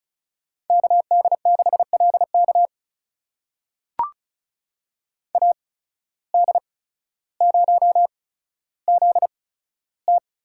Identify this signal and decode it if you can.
Morse code, heard as a keyed tone, about 36 wpm, spaced as for 15 wpm, 700 Hz. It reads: KD6LK AD0ZT
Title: USA Call Signs - 8x Character Spacing - 35wpm